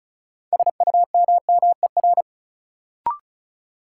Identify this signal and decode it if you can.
Morse code, heard as a keyed tone, about 36 wpm, 700 Hz. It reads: SUMMER
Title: Sentences from Top 1000 Words - Review Part 6 - New Word Drill - 35wpm